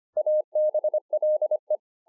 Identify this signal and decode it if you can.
Morse code, heard as a keyed tone, about 26 wpm, 610 Hz. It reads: ABLE
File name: 2011-04-mp3 / BEGINNERS DIARY 25WPM